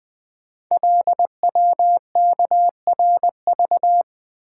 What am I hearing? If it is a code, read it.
Morse code, heard as a keyed tone, about 21 wpm, 700 Hz. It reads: LWKRV
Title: Code Groups - 5 Letters - 20wpm